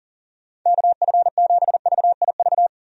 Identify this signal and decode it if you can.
Morse code, heard as a keyed tone, about 37 wpm, 690 Hz. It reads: KF7VIV
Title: USA Call Signs - 6 Characters - 40wpm